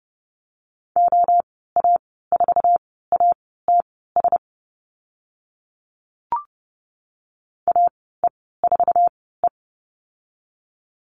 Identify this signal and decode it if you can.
Morse code, heard as a keyed tone, about 31 wpm, spaced as for 17 wpm, 700 Hz. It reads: OA4ATS AE4E
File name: Lesson 15g - Number 5 - Call Signs 30wpm